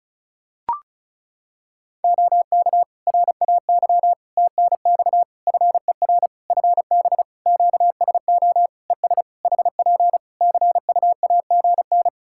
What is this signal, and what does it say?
Morse code, heard as a keyed tone, about 34 wpm, 700 Hz. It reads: OK RAY TNX FER FB QSO ES HP CUAGN
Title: Practice QSO - Exchange 4 - Ending - 35wpm